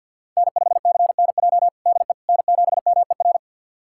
Morse code, heard as a keyed tone, about 51 wpm, 690 Hz. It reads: N5CNJ DE N7MER
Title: USA Callsign DE Callsign - 5 Characters - 50wpm